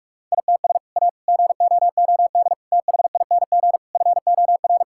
Morse code, heard as a keyed tone, about 42 wpm, 700 Hz. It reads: ITS A GOOD THING FOR
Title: Sentences from Top 1000 Words - Review Part 5 - 45wpm